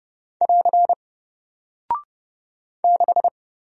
Morse code, heard as a keyed tone, about 30 wpm, 700 Hz. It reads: <AR> 6
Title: Single Character (Letter:Number:Punctuation:Prosign) 30wpm